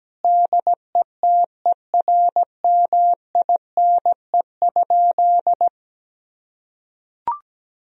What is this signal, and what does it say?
Morse code, heard as a keyed tone, about 17 wpm, 700 Hz. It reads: DETERMINE?
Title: Sentences from Top 1000 Words - Review Part 4 - 17wpm